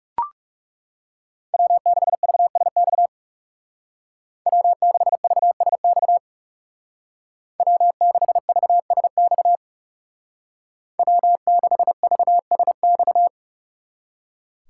Morse code, heard as a keyed tone, about 40 wpm, 700 Hz. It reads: W6VSX W6VSX W6VSX W6VSX